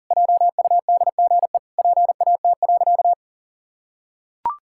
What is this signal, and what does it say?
Morse code, heard as a keyed tone, about 40 wpm, 700 Hz. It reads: JUDGE PAT.
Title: Pangram Challenge v2 - 40wpm